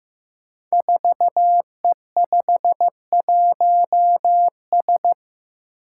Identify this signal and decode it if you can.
Morse code, heard as a keyed tone, about 15 wpm, 700 Hz. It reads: 4E51S
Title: Five Challenging Letter-Number Combinations 15wpm